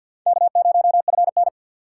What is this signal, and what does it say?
Morse code, heard as a keyed tone, about 46 wpm, 690 Hz. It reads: K0FN
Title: USA Call Signs - 6x Character Spacing - 50wpm